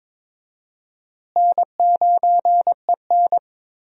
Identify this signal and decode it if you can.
Morse code, heard as a keyed tone, about 22 wpm, 700 Hz. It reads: N9EN
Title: USA Call Signs - 4 Characters - Repeated 3x - 22wpm